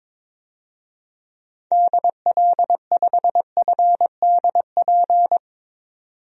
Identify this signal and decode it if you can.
Morse code, heard as a keyed tone, about 21 wpm, 700 Hz. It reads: DL5FDP
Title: DXCC-100 Call Signs - 6 Characters - 22wpm